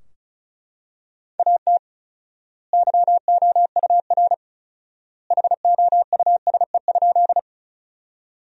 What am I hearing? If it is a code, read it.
Morse code, heard as a keyed tone, about 34 wpm, 700 Hz. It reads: AT YOUR HOUSE?
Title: 4x Word-Spacing - Sentences from Top 200 Words - 35wpm